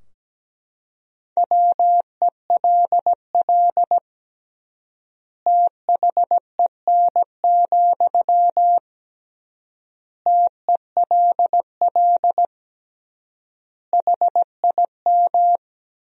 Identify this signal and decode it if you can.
Morse code, heard as a keyed tone, about 17 wpm, 700 Hz. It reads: WELL THEN, TELL HIM
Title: 3x Word-Spacing - Sentences from Top 200 Words - 17wpm